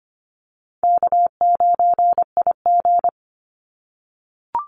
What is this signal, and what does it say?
Morse code, heard as a keyed tone, about 25 wpm, 700 Hz. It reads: K9IG
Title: Call Signs - 25wpm